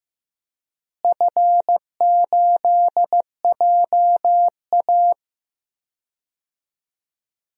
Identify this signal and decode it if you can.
Morse code, heard as a keyed tone, about 15 wpm, 700 Hz. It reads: F8JA
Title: DXCC-100 Call Signs - 4 Characters - Repeated 3x - 15wpm